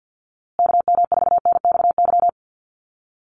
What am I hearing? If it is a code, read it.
Morse code, heard as a keyed tone, about 50 wpm, 700 Hz. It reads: KM4NXY
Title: USA Call Signs Repeated 2x 50wpm